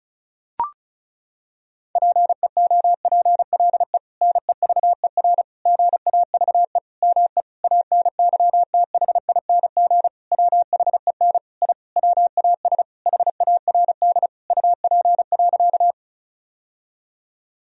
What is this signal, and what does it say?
Morse code, heard as a keyed tone, about 34 wpm, 700 Hz. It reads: PEOPLE NEVER GAVE ME ANYTHING WHEN I WAS HARD UP.
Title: Sentences from Top 1000 Words - Review Part 1 - 35wpm